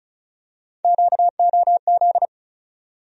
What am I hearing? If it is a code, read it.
Morse code, heard as a keyed tone, about 36 wpm, 700 Hz. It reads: QOZ
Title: USA Call Sign Postfixes - 3 Letters 35wpm